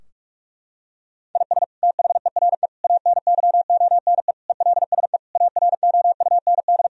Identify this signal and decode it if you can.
Morse code, heard as a keyed tone, about 46 wpm, 700 Hz. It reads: IS THERE ANYONE ELSE AROUND
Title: Sentences from Top 1000 Words - Review Part 3 - 45wpm